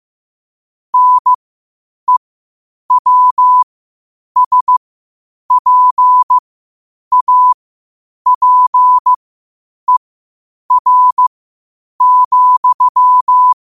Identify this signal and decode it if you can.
Morse code, heard as a keyed tone, about 15 wpm, spaced as for 8 wpm, 1000 Hz. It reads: NEWSPAPER,
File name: qotdfiles / 8wpm